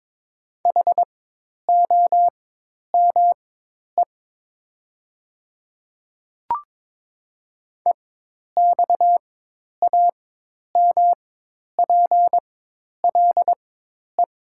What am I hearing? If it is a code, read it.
Morse code, heard as a keyed tone, about 22 wpm, spaced as for 10 wpm, 700 Hz. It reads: HOME EXAMPLE
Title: 4x Character Spacing - Top 500 Words - 22wpm